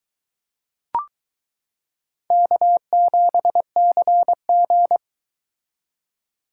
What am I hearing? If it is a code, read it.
Morse code, heard as a keyed tone, about 21 wpm, 700 Hz. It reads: K7CG